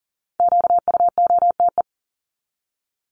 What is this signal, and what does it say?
Morse code, heard as a keyed tone, about 40 wpm, 700 Hz. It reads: QUOTE
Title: Top 1751-2000 Words 40wpm